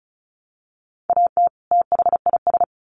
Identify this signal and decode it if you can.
Morse code, heard as a keyed tone, about 34 wpm, 700 Hz. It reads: AT THIS
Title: Sets of 2 Words 35wpm